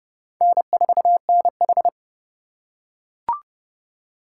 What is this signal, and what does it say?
Morse code, heard as a keyed tone, about 30 wpm, 700 Hz. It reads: N4NH
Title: USA Call Signs - 4x Character Spacing - 30wpm